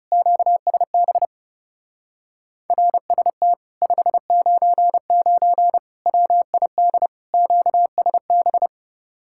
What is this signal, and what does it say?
Morse code, heard as a keyed tone, about 40 wpm, 700 Hz. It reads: QSB RST 599 WID QSB